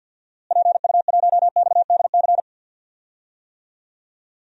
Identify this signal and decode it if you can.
Morse code, heard as a keyed tone, about 52 wpm, 700 Hz. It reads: PU1XDC
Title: DXCC-100 Call Signs - 6 Characters - Repeated 2x - 50wpm